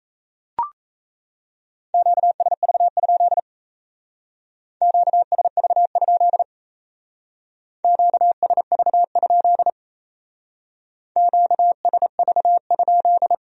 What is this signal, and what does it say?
Morse code, heard as a keyed tone, about 35 wpm, 700 Hz. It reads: QSV? QSV? QSV? QSV?
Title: Speed-Racing Q-Codes - 28wpm